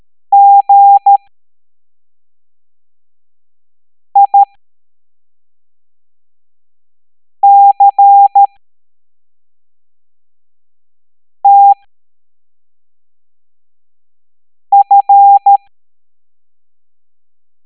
Morse code, heard as a keyed tone, about 13 wpm, spaced as for 3 wpm, 800 Hz. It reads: GICTF